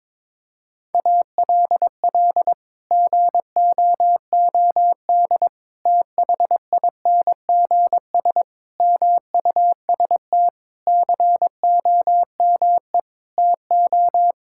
Morse code, heard as a keyed tone, about 22 wpm, 700 Hz. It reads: ALL GOOD THINGS MUST COME TO
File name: Sets of 9 Words - 22wpm